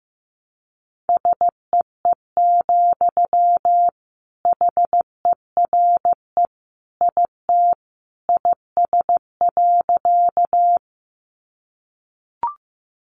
Morse code, heard as a keyed tone, about 15 wpm, 700 Hz. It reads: SEE, HERE IT IS.
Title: Sentences from Top 200 Words 15wpm